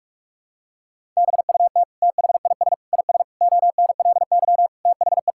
Morse code, heard as a keyed tone, about 46 wpm, 700 Hz. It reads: BUT THIS IS ONLY THE
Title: Sentences from Top 1000 Words - Review Part 5 - 45wpm